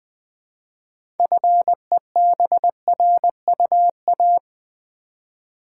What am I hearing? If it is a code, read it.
Morse code, heard as a keyed tone, about 20 wpm, 700 Hz. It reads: FEBRUA
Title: Build A Word - 3 Letters At A Time - 8-Letter Words - 20wpm